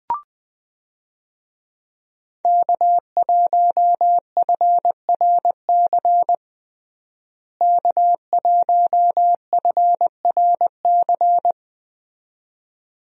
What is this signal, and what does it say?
Morse code, heard as a keyed tone, about 20 wpm, 700 Hz. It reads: K1FRC K1FRC
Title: USA Call Signs - 5 Characters - Repeated 2x - 20wpm